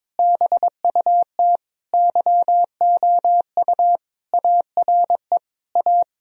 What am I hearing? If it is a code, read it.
Morse code, heard as a keyed tone, about 22 wpm, 700 Hz. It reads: BUT YOU ARE A
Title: Sentences from Top 1000 Words - Review Part 1 - 22wpm